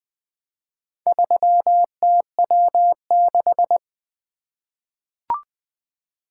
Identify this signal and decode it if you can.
Morse code, heard as a keyed tone, about 20 wpm, 700 Hz. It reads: 3TW6
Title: Four Letter-Number Combinations - 20wpm